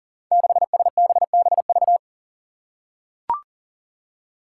Morse code, heard as a keyed tone, about 39 wpm, 700 Hz. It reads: 6SBBV
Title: Five Challenging Letter-Number Combinations 40wpm